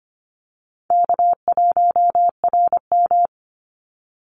Morse code, heard as a keyed tone, about 25 wpm, 700 Hz. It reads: K1RM
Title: Call Signs - 25wpm